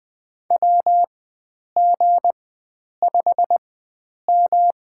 Morse code, heard as a keyed tone, about 20 wpm, spaced as for 9 wpm, 700 Hz. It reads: WG5M
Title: USA Call Signs - 4x Character Spacing - 20wpm